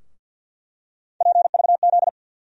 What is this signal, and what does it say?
Morse code, heard as a keyed tone, about 50 wpm, 700 Hz. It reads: PVZ
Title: USA Call Sign Postfixes - 3 Letters 50wpm